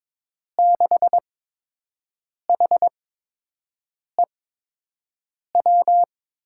Morse code, heard as a keyed tone, about 22 wpm, spaced as for 6 wpm, 700 Hz. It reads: 6HEW
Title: USA Call Signs - 6 Characters - 8x Character Spacing - 22wpm